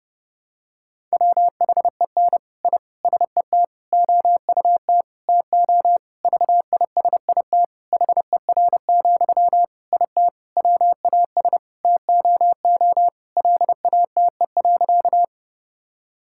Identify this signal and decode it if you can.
Morse code, heard as a keyed tone, about 30 wpm, 700 Hz. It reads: WHEN I SET OUT TO VISIT HER, IT WAS TOO LATE.
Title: Sentences from Top 1000 Words - Review Part 6 - 30wpm